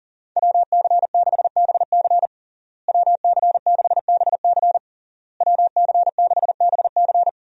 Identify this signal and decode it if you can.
Morse code, heard as a keyed tone, about 40 wpm, 700 Hz. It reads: WC6BC WC6BC WC6BC
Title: USA Call Signs - 5 Characters - Repeated 3x - 40wpm